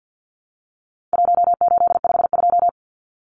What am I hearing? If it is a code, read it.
Morse code, heard as a keyed tone, about 50 wpm, 700 Hz. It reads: 1852
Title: Numbers - 4-Digit 50wpm